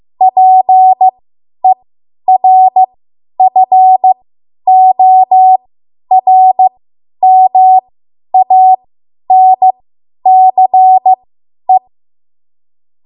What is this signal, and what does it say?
Morse code, heard as a keyed tone, about 15 wpm, spaced as for 10 wpm, 750 Hz. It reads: PERFORMANCE
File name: Morse / 10WPM